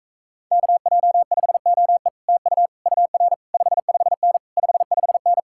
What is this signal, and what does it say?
Morse code, heard as a keyed tone, about 45 wpm, 700 Hz. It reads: KJ5OE TU UR 55N 55N